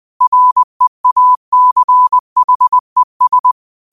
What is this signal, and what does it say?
Morse code, heard as a keyed tone, about 20 wpm, 1000 Hz. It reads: REACHES
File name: qotdfiles / 20wpm